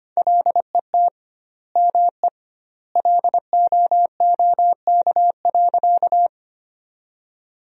Morse code, heard as a keyed tone, about 25 wpm, 700 Hz. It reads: LET ME LOOK.
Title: 2x Word-Spacing - Sentences from Top 300 Words - 25wpm